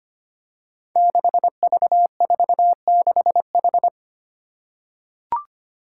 Morse code, heard as a keyed tone, about 25 wpm, 700 Hz. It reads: 6V46H
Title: Five Challenging Letter-Number Combinations 25wpm